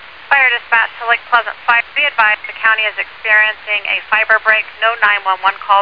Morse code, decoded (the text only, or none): none